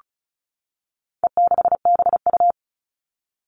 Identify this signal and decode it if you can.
Morse code, heard as a keyed tone, about 34 wpm, 700 Hz. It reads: E6BU
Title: All Cut Numbers - 4-Digit 35wpm